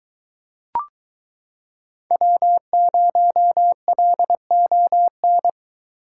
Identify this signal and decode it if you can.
Morse code, heard as a keyed tone, about 22 wpm, 700 Hz. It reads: W0LON